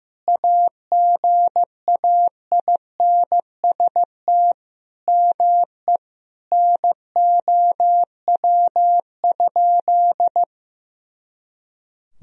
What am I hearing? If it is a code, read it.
Morse code, heard as a keyed tone, about 15 wpm, 700 Hz. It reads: AGAINST ME NOW?
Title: Sentences from Top 400 Words and Abbreviations 15wpm